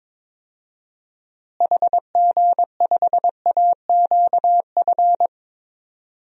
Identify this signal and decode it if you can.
Morse code, heard as a keyed tone, about 22 wpm, 700 Hz. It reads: HG5AQF
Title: DXCC-100 Call Signs - 6 Characters - Repeated 3x - 22wpm